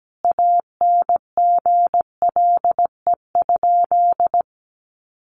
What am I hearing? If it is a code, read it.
Morse code, heard as a keyed tone, about 17 wpm, 700 Hz. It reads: ANGLE?
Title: Sentences from Top 1500 Words 17wpm